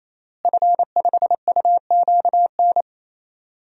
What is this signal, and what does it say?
Morse code, heard as a keyed tone, about 28 wpm, 700 Hz. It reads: F5UQN
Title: DXCC-100 Call Signs - 5 Characters - Repeated 3x - 28wpm